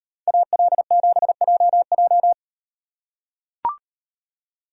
Morse code, heard as a keyed tone, about 38 wpm, 700 Hz. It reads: AL7JJ